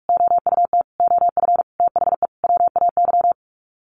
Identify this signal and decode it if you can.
Morse code, heard as a keyed tone, about 44 wpm, 700 Hz. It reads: OUT OF THE WAY
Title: Sets of 5 Words Encore 45wpm